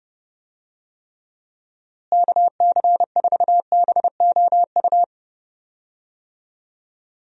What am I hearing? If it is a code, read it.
Morse code, heard as a keyed tone, about 30 wpm, 700 Hz. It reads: KC4BOU
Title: USA Call Signs - 6 Characters - Repeated 2x - 30wpm